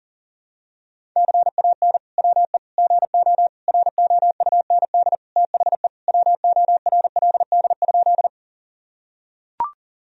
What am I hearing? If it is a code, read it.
Morse code, heard as a keyed tone, about 40 wpm, 700 Hz. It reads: CAN WE GO ROUND THE WORLD?
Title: Sentences from Top 1000 Words - Review Part 1 - 40wpm